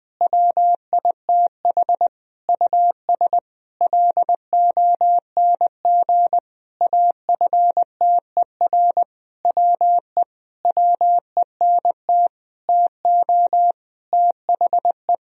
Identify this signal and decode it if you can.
Morse code, heard as a keyed tone, about 20 wpm, 700 Hz. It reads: WITH US LONG AFTER WE WENT TO THE
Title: Sentences from Top 1000 Words - Review Part 2 - 20wpm